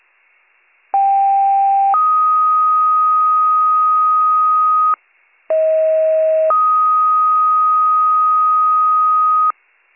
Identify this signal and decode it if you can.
Two-tone page, heard as a one-second tone, then a three-second tone, about 790 then 1250 Hz, about 640 then 1250 Hz. It is heard from 0.9 to 9.5 s.